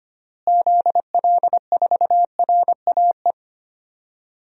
Morse code, heard as a keyed tone, about 25 wpm, 700 Hz. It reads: ZL4RAE
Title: DXCC-100 Call Signs - 6 Characters - Repeated 3x - 25wpm